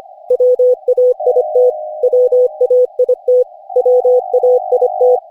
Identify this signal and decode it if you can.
Morse code, heard as a keyed tone, about 25 wpm, 500 Hz. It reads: WAIT WAIT WAIT
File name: cw / Session5 cw training